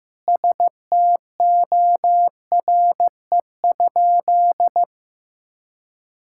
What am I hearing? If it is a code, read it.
Morse code, heard as a keyed tone, about 15 wpm, 700 Hz. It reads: STORE?
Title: Sentences from Top 1000 Words - Review Part 5 - 15wpm